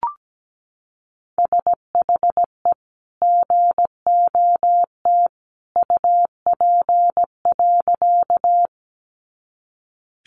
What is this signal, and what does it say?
Morse code, heard as a keyed tone, about 17 wpm, 700 Hz. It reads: SHE GOT UP.
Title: Sentences from Top 300 Words 17wpm